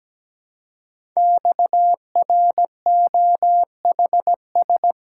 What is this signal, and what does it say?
Morse code, heard as a keyed tone, about 17 wpm, 700 Hz. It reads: XROHS
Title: Code Groups - 5 Letters - 17wpm